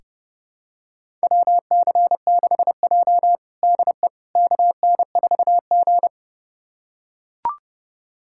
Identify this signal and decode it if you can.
Morse code, heard as a keyed tone, about 30 wpm, 700 Hz. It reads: WC6J DE KN4G
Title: USA Callsign DE Callsign - 4 Characters - 30wpm